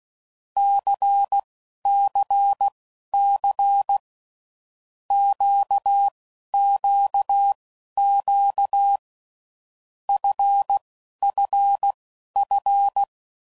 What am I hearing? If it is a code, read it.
Morse code, heard as a keyed tone, about 16 wpm, spaced as for 11 wpm, 800 Hz. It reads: CCC QQQ FFF
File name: Groups / Basic Letter Groups / LFCYQ